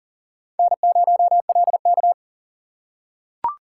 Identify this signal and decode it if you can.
Morse code, heard as a keyed tone, about 40 wpm, 700 Hz. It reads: N0LK
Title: USA Call Signs - 4 Characters - 3x Character Spacing - 40wpm